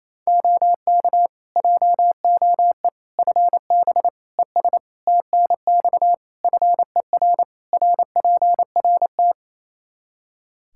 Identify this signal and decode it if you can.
Morse code, heard as a keyed tone, about 28 wpm, 700 Hz. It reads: OK JOE FB ES TNX FER RPRT
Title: K1USN SST QSO Practice 28wpm